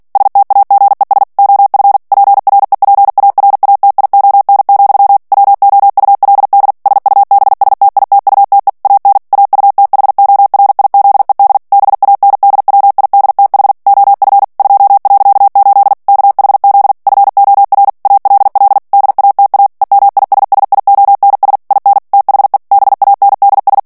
Morse code, heard as a keyed tone, about 48 wpm, 800 Hz. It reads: STAGES OF PREPARATION, WOULD SUBSTITUTE AN AUTHORIZED BANDWIDTH OF 2.8 KHZ FOR ALL DATA EMISSIONS IN THE BANDS
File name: ANGLAIS texte clair 3 vitesse 45 mots minute